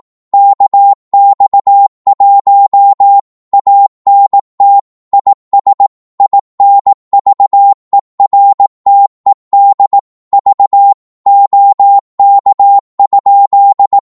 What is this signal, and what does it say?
Morse code, heard as a keyed tone, about 18 wpm, 800 Hz. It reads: KX1 ANT IS INVERTED V OK?
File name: Morse / qso-18wpm